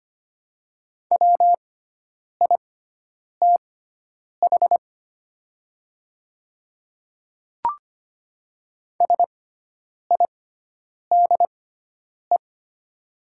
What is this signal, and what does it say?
Morse code, heard as a keyed tone, about 25 wpm, spaced as for 9 wpm, 690 Hz. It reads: WITH SIDE
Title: 6x Character Spacing - Top 100 Words - 25wpm